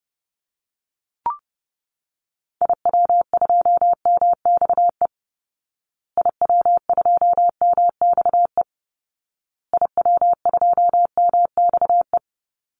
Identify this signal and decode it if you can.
Morse code, heard as a keyed tone, about 30 wpm, 700 Hz. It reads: IW2MXE IW2MXE IW2MXE
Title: Call Signs Repeated 2x - 30wpm